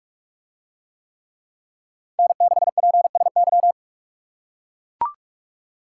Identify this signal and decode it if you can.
Morse code, heard as a keyed tone, about 40 wpm, 700 Hz. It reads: N6PSY